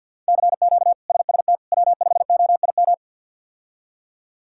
Morse code, heard as a keyed tone, about 52 wpm, 690 Hz. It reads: CQ SST W5OIM
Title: K1USN SST QSO Practice - 50wpm